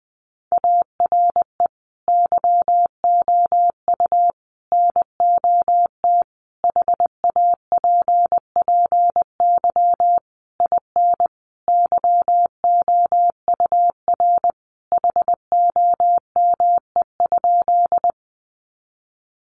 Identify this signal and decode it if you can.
Morse code, heard as a keyed tone, about 20 wpm, 700 Hz. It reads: ARE YOU NOT HAPPY IN YOUR HOME?
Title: Sentences from Top 1000 Words 20wpm